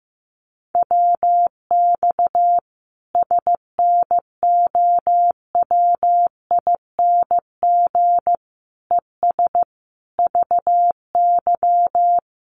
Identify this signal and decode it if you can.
Morse code, heard as a keyed tone, about 15 wpm, 700 Hz. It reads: WX SNOWING ES VY
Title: Common QSO Phrases 15wpm